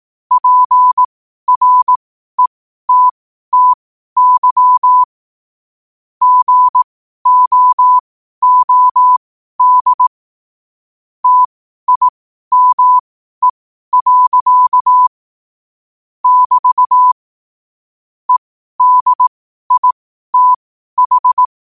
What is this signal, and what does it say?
Morse code, heard as a keyed tone, about 18 wpm, spaced as for 12 wpm, 1000 Hz. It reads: PRETTY GOOD TIME. <BT> EDITH
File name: qotdfiles / 12wpm